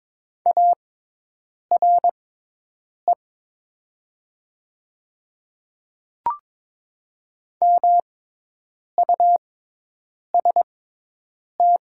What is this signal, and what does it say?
Morse code, heard as a keyed tone, about 22 wpm, spaced as for 8 wpm, 700 Hz. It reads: ARE MUST
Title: 6x Character Spacing - Top 500 Words - 22wpm